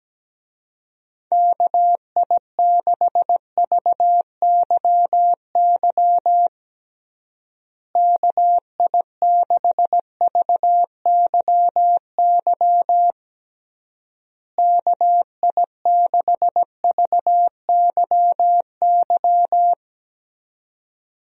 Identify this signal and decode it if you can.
Morse code, heard as a keyed tone, about 17 wpm, 700 Hz. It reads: KI6VYY KI6VYY KI6VYY
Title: USA Call Signs - 6 Characters - Repeated 3x - 17wpm